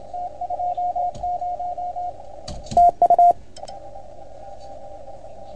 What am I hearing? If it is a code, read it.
Morse code, heard as a keyed tone, about 28 wpm, 690 Hz. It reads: T20 TU
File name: Archiv / CUC310 / OK1FGD CUC 310 pileup